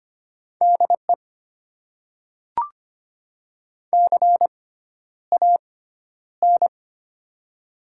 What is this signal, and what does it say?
Morse code, heard as a keyed tone, about 25 wpm, 700 Hz. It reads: D E C A N
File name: Top 90 QSO Elements - 6x Character Spacing - 25wpm